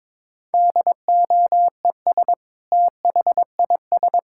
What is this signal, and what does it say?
Morse code, heard as a keyed tone, about 22 wpm, 700 Hz. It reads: DOES THIS
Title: Sentences from Top 1000 Words - Review Part 5 - 22wpm